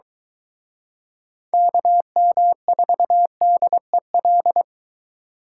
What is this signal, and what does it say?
Morse code, heard as a keyed tone, about 23 wpm, 700 Hz. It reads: KM4DEL